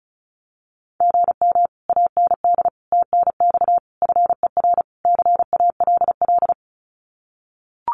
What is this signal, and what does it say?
Morse code, heard as a keyed tone, about 36 wpm, 700 Hz. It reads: GM AND TNX FER CALL
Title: Common QSO Phrases Encore 35wpm